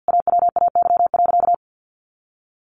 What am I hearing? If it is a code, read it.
Morse code, heard as a keyed tone, about 50 wpm, 700 Hz. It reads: AWAY.
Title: Sentences from Top 1100 Words 50wpm